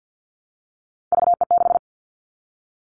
Morse code, heard as a keyed tone, about 50 wpm, 700 Hz. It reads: VE6
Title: Callsign Prefixes for Top 25 DXCC - 50wpm